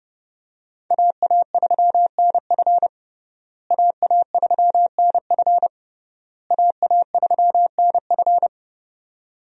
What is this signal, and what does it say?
Morse code, heard as a keyed tone, about 30 wpm, 700 Hz. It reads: AA3NF AA3NF AA3NF
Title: USA Call Signs - 5 Characters - Repeated 3x - 30wpm